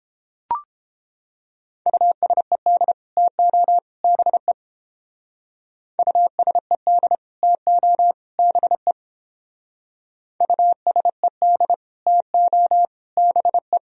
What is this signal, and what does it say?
Morse code, heard as a keyed tone, about 30 wpm, 700 Hz. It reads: USED TO BE USED TO BE USED TO BE